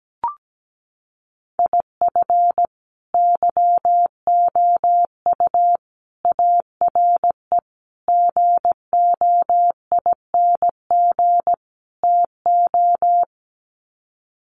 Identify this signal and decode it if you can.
Morse code, heard as a keyed tone, about 17 wpm, 700 Hz. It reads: IF YOU ARE GOING TO
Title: Sets of 5 Words Encore2 17wpm